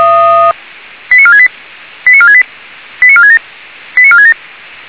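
Five-tone selective call (ZVEI), heard as quick runs of five tones, tones of about 70 ms. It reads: 89468 89468 89467 89467
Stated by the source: ZVEI five-tone selective call received by SDR.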